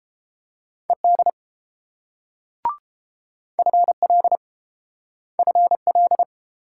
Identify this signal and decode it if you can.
Morse code, heard as a keyed tone, about 29 wpm, 700 Hz. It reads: ED FL FL